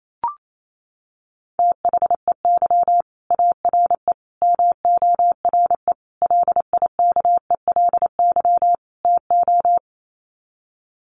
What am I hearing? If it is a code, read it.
Morse code, heard as a keyed tone, about 28 wpm, 700 Hz. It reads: THEY ARE MORE LIKELY TO
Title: Sets of 5 Words Encore2 28wpm